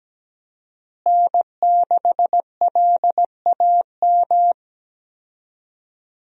Morse code, heard as a keyed tone, about 17 wpm, 700 Hz. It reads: N6LAM
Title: USA Call Signs - 5 Characters - Repeated 3x - 17wpm